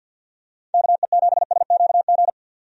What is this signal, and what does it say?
Morse code, heard as a keyed tone, about 47 wpm, 700 Hz. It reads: KE7SQG
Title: USA Call Signs - 6 Characters - 50wpm